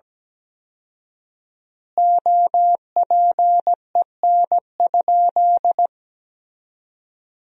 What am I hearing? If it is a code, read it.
Morse code, heard as a keyed tone, about 17 wpm, 700 Hz. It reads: OPEN?
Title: 4x Word-Spacing - Sentences from Top 300 Words - 17wpm